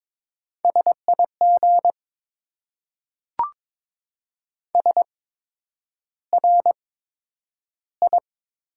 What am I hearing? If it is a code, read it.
Morse code, heard as a keyed tone, about 22 wpm, spaced as for 9 wpm, 700 Hz. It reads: SIG SRI
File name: Top 90 QSO Elements - 8x Character Spacing - 22wpm